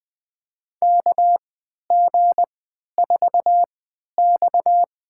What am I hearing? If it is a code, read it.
Morse code, heard as a keyed tone, about 20 wpm, spaced as for 11 wpm, 700 Hz. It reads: KG4X
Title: USA Call Signs - 4 Characters - 3x Character Spacing - 20wpm